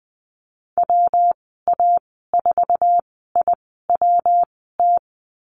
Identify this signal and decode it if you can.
Morse code, heard as a keyed tone, about 20 wpm, spaced as for 15 wpm, 700 Hz. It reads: WA4IWT
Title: 122 Lesson 20g - Number 3 - Call Signs 20wpm